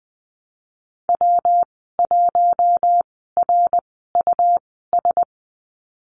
Morse code, heard as a keyed tone, about 20 wpm, spaced as for 14 wpm, 700 Hz. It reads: W1RUS
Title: 154 Lesson 25g - Letter p - Call Signs 20wpm